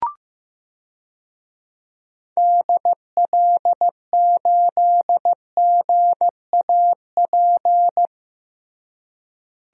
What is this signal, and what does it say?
Morse code, heard as a keyed tone, about 16 wpm, 700 Hz. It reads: DL8GAP
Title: DXCC-100 Call Signs - 6 Characters - Repeated 2x - 15wpm